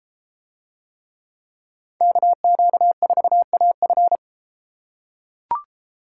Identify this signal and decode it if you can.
Morse code, heard as a keyed tone, about 34 wpm, 700 Hz. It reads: KQ4AF